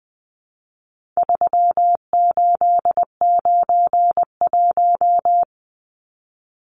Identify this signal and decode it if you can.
Morse code, heard as a keyed tone, about 20 wpm, 700 Hz. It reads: 3891
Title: Numbers - 4-Digit 20wpm